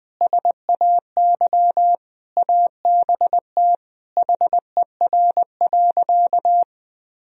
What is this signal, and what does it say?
Morse code, heard as a keyed tone, about 20 wpm, 700 Hz. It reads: SAY ABT HER.
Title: Sentences from Top 200 Words and Abbreviations 20wpm